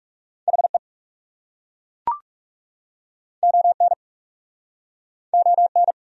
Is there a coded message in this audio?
Yes: Morse code from 0.5 to 5.9 s, heard as a keyed tone, about 40 wpm, 700 Hz. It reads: HE ON ON